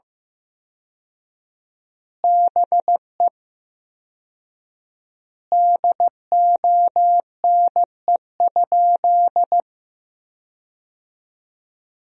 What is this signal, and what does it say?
Morse code, heard as a keyed tone, about 15 wpm, 700 Hz. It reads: BE DONE?
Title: 4x Word-Spacing - Sentences from Top 500 Words - 15wpm